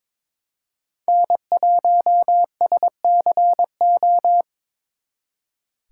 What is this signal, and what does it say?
Morse code, heard as a keyed tone, about 26 wpm, 700 Hz. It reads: N1SCO
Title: Speed Racing - USA 1x3 Call Signs 22wpm